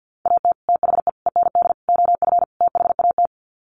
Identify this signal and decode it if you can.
Morse code, heard as a keyed tone, about 50 wpm, 700 Hz. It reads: AT THE END OF THAT
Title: Sets of 5 Words Encore2 50wpm